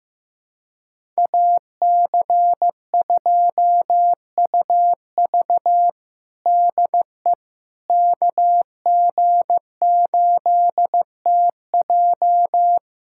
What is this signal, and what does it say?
Morse code, heard as a keyed tone, about 16 wpm, 700 Hz. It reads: AC2UV DE KG8TJ
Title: USA Callsign DE Callsign - 5 Characters - 15wpm